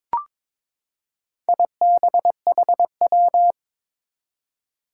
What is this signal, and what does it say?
Morse code, heard as a keyed tone, about 22 wpm, 700 Hz. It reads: IBHW
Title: Four Challenging Letter-Number Combinations 22wpm